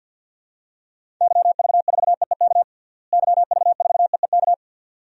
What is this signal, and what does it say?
Morse code, heard as a keyed tone, about 50 wpm, 700 Hz. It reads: YV4EEK YV4EEK
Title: DXCC-100 Call Signs - 6 Characters - Repeated 2x - 50wpm